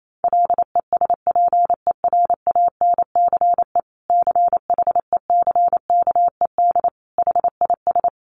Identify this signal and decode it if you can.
Morse code, heard as a keyed tone, about 28 wpm, 700 Hz. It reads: LESPERANCE CHECKED HIS
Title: A Sound Of Thunder 28wpm